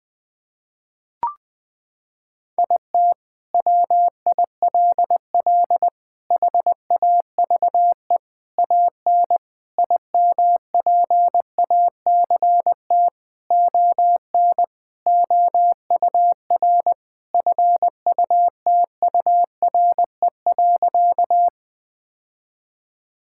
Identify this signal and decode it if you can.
Morse code, heard as a keyed tone, about 20 wpm, 700 Hz. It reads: IT WILL HAVE AN IMPACT ON OUR FUTURE.
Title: Sentences from Top 1000 Words - Review Part 6 - 20wpm